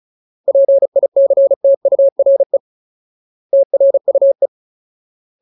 Morse code, heard as a keyed tone, about 34 wpm, 550 Hz. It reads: PICTURE TRUE
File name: Ultimate Word List / 35WPM Ultimate